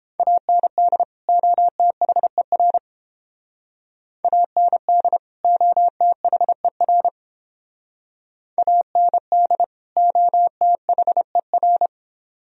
Morse code, heard as a keyed tone, about 31 wpm, 700 Hz. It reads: AND OTHER AND OTHER AND OTHER